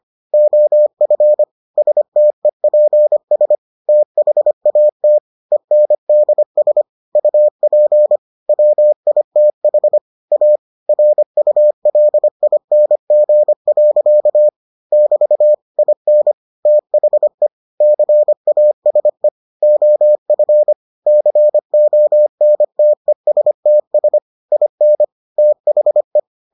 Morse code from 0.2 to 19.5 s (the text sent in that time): OF STEPS THAT ENDS UP WITH A RULING. <BT> IN THE CASE